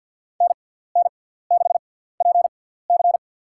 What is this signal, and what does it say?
Morse code, heard as a keyed tone, about 53 wpm, spaced as for 17 wpm, 700 Hz. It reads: NN6PC
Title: USA Call Signs - 5 Characters - 6x Character Spacing - 50wpm